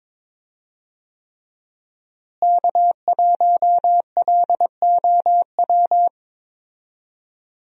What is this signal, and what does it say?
Morse code, heard as a keyed tone, about 21 wpm, 700 Hz. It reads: K1LOW
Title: USA Call Signs - 5 Characters - Repeated 3x - 22wpm